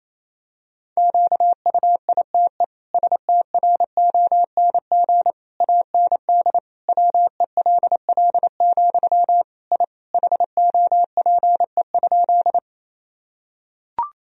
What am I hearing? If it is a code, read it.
Morse code, heard as a keyed tone, about 27 wpm, 700 Hz. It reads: QUITE STRONG AND WELL, I HOPE?
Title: Sentences from Top 1000 Words - Review Part 3 - 28wpm